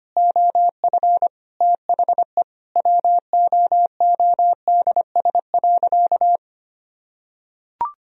Morse code, heard as a keyed tone, about 25 wpm, 700 Hz. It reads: OF THE WOODS.
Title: Sentences from Top 1000 Words - Review Part 2 - 25wpm